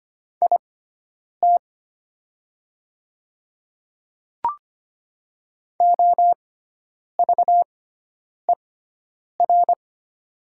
Morse code, heard as a keyed tone, about 25 wpm, spaced as for 9 wpm, 700 Hz. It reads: IT OVER
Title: 6x Character Spacing - Top 100 Words - 25wpm